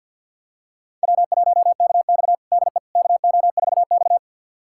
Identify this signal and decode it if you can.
Morse code, heard as a keyed tone, about 51 wpm, 700 Hz. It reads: W1KX DE KO4X
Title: USA Callsign DE Callsign - 4 Characters - 50wpm